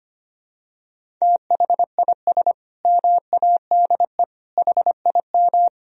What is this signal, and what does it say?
Morse code, heard as a keyed tone, about 25 wpm, 700 Hz. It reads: THIS MADE HIM
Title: Sentences from Top 500 Words and Abbreviations 25wpm